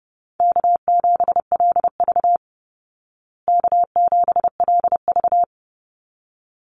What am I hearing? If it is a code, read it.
Morse code, heard as a keyed tone, about 30 wpm, 700 Hz. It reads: K7LV K7LV
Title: Easy Call Signs Repeated 2x - 30wpm